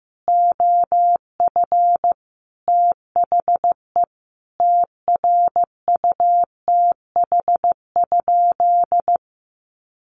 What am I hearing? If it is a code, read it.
Morse code, heard as a keyed tone, about 15 wpm, 700 Hz. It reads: OF THE TRUTH?
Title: Sentences from Top 1400 Words 15wpm